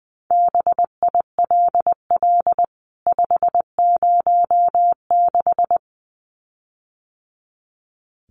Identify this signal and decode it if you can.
Morse code, heard as a keyed tone, about 20 wpm, 700 Hz. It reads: BILL 506
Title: CWT Exchange 20wpm